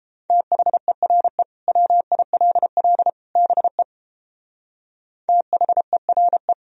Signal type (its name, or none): Morse code